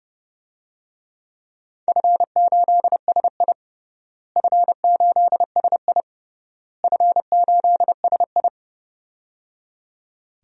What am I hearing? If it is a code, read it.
Morse code, heard as a keyed tone, about 30 wpm, 700 Hz. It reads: F8SI F8SI F8SI
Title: DXCC-100 Call Signs - 4 Characters - Repeated 3x - 30wpm